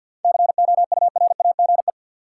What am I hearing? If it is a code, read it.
Morse code, heard as a keyed tone, about 51 wpm, 700 Hz. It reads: COURAGE
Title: Seven Letter Words - 50wpm